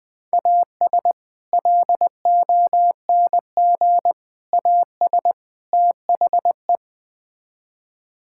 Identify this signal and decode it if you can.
Morse code, heard as a keyed tone, about 20 wpm, 690 Hz. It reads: AS LONG AS THE
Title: Build A Phrase - 4-Word Phrases - 20wpm